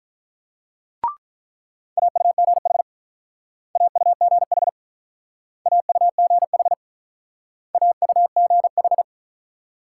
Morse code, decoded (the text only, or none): AUGH AUGH AUGH AUGH